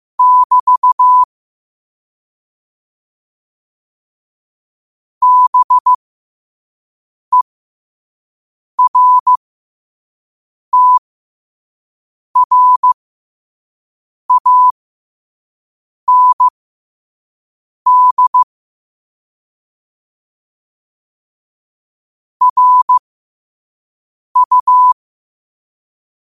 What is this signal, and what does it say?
Morse code, heard as a keyed tone, about 15 wpm, spaced as for 5 wpm, 1000 Hz. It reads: <BT> BERTRAND RU
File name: 2024-10-07-5wpm